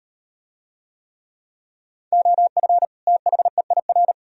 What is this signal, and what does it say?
Morse code, heard as a keyed tone, about 38 wpm, 700 Hz. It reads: OF THEIR